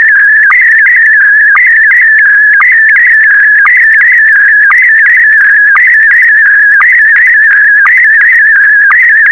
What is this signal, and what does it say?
SSTV (slow-scan television), Scottie DX.